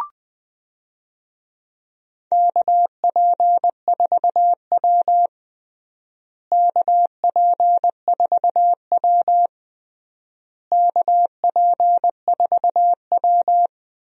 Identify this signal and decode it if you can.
Morse code, heard as a keyed tone, about 21 wpm, 700 Hz. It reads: KP4W KP4W KP4W
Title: DXCC-100 Call Signs - 4 Characters - Repeated 3x - 20wpm